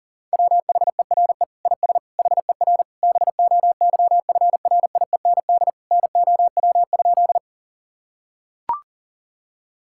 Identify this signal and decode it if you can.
Morse code, heard as a keyed tone, about 40 wpm, 700 Hz. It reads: WHERE IS HER BOYFRIEND NOW?
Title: Sentences from Top 1000 Words - Review Part 5 - 40wpm